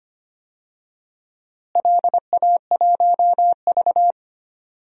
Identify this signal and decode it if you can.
Morse code, heard as a keyed tone, about 25 wpm, 700 Hz. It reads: LA1V
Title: DXCC-100 Call Signs - 4 Characters - Repeated 3x - 25wpm